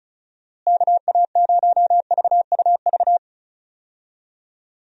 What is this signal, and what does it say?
Morse code, heard as a keyed tone, about 34 wpm, 700 Hz. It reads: KA0VUV
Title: USA Call Signs - 6 Characters - Repeated 2x - 35wpm